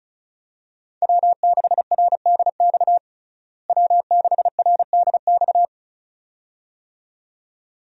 Morse code, heard as a keyed tone, about 36 wpm, 700 Hz. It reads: W6RDX W6RDX
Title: USA Call Signs - 5 Characters - Repeated 2x - 35wpm